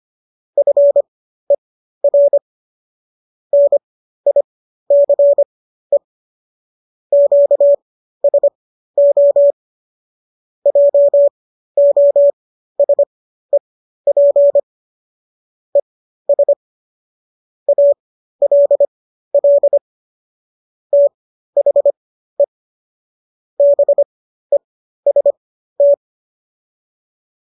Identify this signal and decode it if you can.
Morse code, heard as a keyed tone, about 25 wpm, spaced as for 13 wpm, 570 Hz. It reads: FER NICE QSO JOSEP ES ALL THE BEST
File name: qso214 13